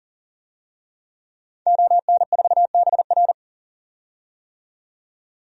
Morse code, heard as a keyed tone, about 40 wpm, 700 Hz. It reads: ON4BR
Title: DXCC-100 Call Signs - 5 Characters - 40wpm